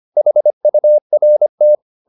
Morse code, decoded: HURT

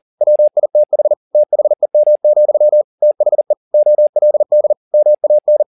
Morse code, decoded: WITH THEM, THE OLD MAN

